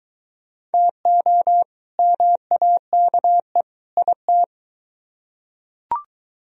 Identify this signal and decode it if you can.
Morse code, heard as a keyed tone, about 22 wpm, 700 Hz. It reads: TO MAKE IT